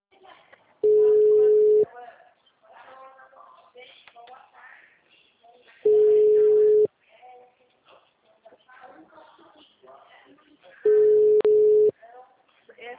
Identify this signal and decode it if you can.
Ringback tone (Europe, 425 Hz), with speech.
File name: out- -2010- - - .90758